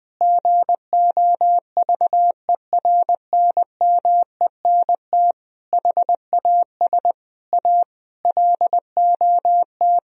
Morse code, heard as a keyed tone, about 20 wpm, 700 Hz. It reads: GOVERNMENT HAS A LOT
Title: Sentences from Top 1000 Words - Review Part 4 - 20wpm